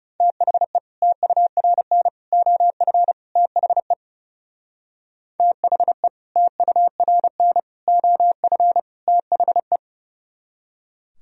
Morse code, heard as a keyed tone, about 37 wpm, 690 Hz. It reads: THE TURN OF THE THE TURN OF THE
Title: Speed-Racing - Sets of 4 Words 30wpm